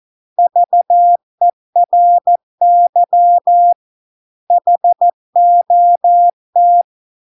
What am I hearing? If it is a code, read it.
Morse code, heard as a keyed tone, about 14 wpm, 700 Hz. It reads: VERY HOT